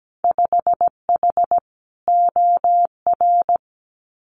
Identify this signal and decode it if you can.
Morse code, heard as a keyed tone, about 17 wpm, 700 Hz. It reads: 5H OR
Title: Winter Field Day Exchanges Encore 17wpm